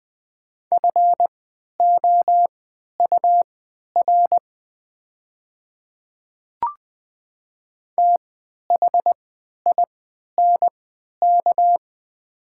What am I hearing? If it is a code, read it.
Morse code, heard as a keyed tone, about 21 wpm, spaced as for 11 wpm, 700 Hz. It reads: FOUR THINK
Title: 3x Character Spacing - Top 300 Words - 20wpm